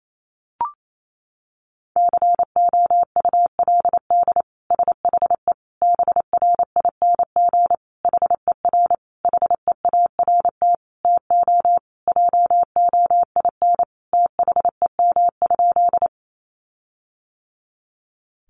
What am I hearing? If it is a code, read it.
Morse code, heard as a keyed tone, about 28 wpm, 700 Hz. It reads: COULD SHE BRING HER HEART TO JOIN THEM?
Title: Sentences from Top 1400 Words 28wpm